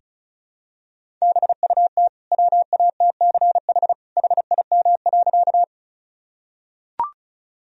Morse code, heard as a keyed tone, about 36 wpm, 700 Hz. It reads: BUT WATCH HIM.
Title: Sentences from Top 1000 Words - Review Part 1 - 35wpm